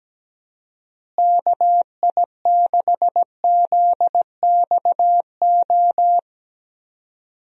Morse code, heard as a keyed tone, about 17 wpm, 700 Hz. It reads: KI6ZXO
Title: USA Call Signs - 6 Characters - Repeated 3x - 17wpm